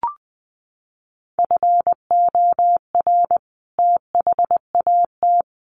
Morse code, heard as a keyed tone, about 20 wpm, 700 Hz. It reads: FOR THAT
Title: Sentences from Top 1500 Words 20wpm